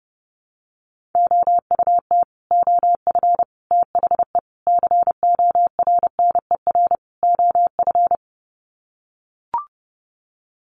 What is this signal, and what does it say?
Morse code, heard as a keyed tone, about 30 wpm, 700 Hz. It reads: OUT OF THE CORNER OF
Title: Sets of 5 Words 30wpm